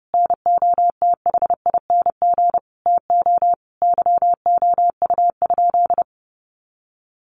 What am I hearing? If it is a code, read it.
Morse code, heard as a keyed tone, about 30 wpm, 700 Hz. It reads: NOTHING TO YOU?
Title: Sentences from Top 1300 Words 30wpm